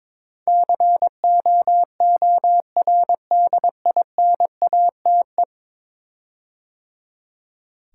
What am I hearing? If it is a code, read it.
Morse code, heard as a keyed tone, about 23 wpm, 700 Hz. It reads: COORDINATE
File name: practice / 22wpm / 10 Letter Words - First 3 Letters Given - 22wpm